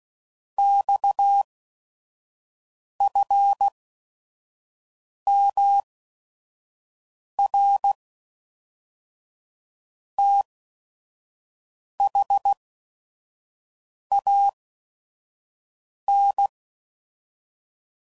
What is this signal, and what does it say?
Morse code, heard as a keyed tone, about 16 wpm, spaced as for 5 wpm, 800 Hz. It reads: XFMR THAN
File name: CW Code Groups @5WPM / Technical Subjects / SwPwrSup